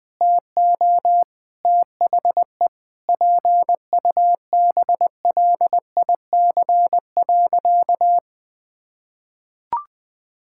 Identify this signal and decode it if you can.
Morse code, heard as a keyed tone, about 20 wpm, 700 Hz. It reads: TO THE PUBLIC.
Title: Sentences with 7-Word Phrases - 20wpm